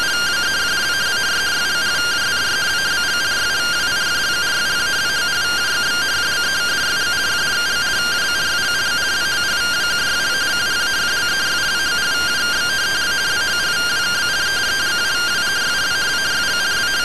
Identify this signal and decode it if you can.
Two-tone teleprinter signal, about 170 Hz shift.